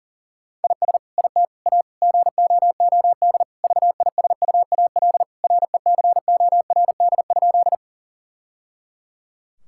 Morse code, heard as a keyed tone, about 40 wpm, 700 Hz. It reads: IS IT A GOOD VISUAL RECORD?
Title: Sentences from Top 1000 Words - Review Part 6 - 40wpm